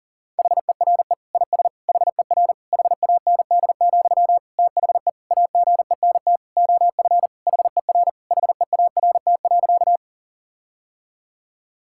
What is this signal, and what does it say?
Morse code, heard as a keyed tone, about 40 wpm, 700 Hz. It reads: HERE IS HER HAND, THE AGENT OF HER HEART.
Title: Sentences from Top 1600 Words 40wpm